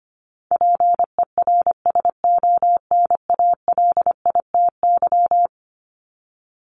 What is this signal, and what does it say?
Morse code, heard as a keyed tone, about 25 wpm, 700 Hz. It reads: PERSONALITY
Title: Top 1251-1500 Words 25wpm